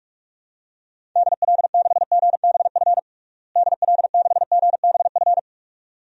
Morse code, heard as a keyed tone, about 46 wpm, 700 Hz. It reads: DL6GBF DL6GBF
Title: DXCC-100 Call Signs - 6 Characters - Repeated 2x - 45wpm